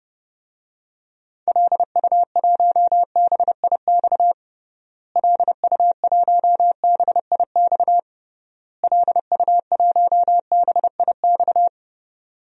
Morse code, heard as a keyed tone, about 30 wpm, 700 Hz. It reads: LU1BIX LU1BIX LU1BIX
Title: DXCC-100 Call Signs - 6 Characters - Repeated 3x - 30wpm